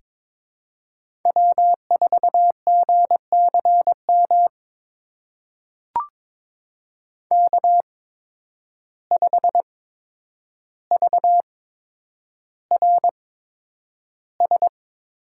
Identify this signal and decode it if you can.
Morse code, heard as a keyed tone, about 21 wpm, spaced as for 9 wpm, 700 Hz. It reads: W4GCM K5VRS
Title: USA Call Signs - 5 Characters - 8x Character Spacing - 22wpm